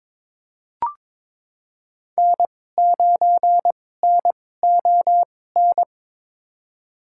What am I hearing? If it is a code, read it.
Morse code, heard as a keyed tone, about 21 wpm, spaced as for 16 wpm, 690 Hz. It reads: N9NON
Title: USA Call Signs - 5 Characters - 2x Character Spacing - 22wpm